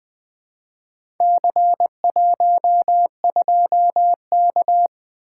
Morse code, heard as a keyed tone, about 20 wpm, 700 Hz. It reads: C12K
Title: Four Letter-Number Combinations - 20wpm